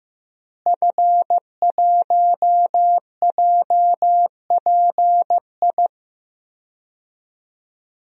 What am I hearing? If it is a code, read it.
Morse code, heard as a keyed tone, about 15 wpm, 700 Hz. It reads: F1JPI
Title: DXCC-100 Call Signs - 5 Characters - Repeated 2x - 15wpm